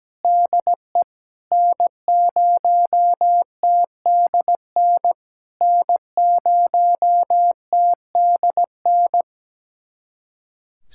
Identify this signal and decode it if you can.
Morse code, heard as a keyed tone, about 17 wpm, 700 Hz. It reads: DE N0TDN N0TDN
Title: SOTA QSO Practice 17wpm